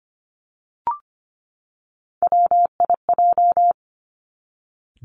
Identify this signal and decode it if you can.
Morse code, heard as a keyed tone, about 25 wpm, 700 Hz. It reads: WIJ
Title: Three Challenging Letter-Number Combinations 25wpm